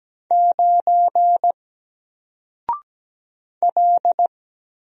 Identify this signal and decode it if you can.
Morse code, heard as a keyed tone, about 17 wpm, 700 Hz. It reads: 9 L
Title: Single Character (Letter:Number:Punctuation:Prosign) 17wpm